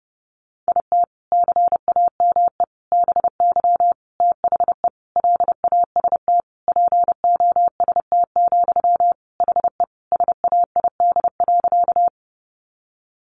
Morse code, heard as a keyed tone, about 30 wpm, 700 Hz. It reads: IT CAME BY THE LAST POST, HE SAID.
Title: Sentences from Top 1000 Words 30wpm